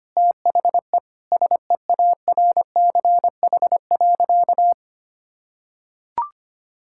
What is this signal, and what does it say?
Morse code, heard as a keyed tone, about 25 wpm, 700 Hz. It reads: THE SEARCH.
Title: Sentences from Top 1000 Words - Review Part 5 - 25wpm